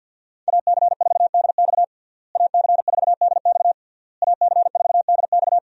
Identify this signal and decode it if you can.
Morse code, heard as a keyed tone, about 52 wpm, 700 Hz. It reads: AC4DX AC4DX AC4DX
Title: USA Call Signs - 5 Characters - Repeated 3x - 50wpm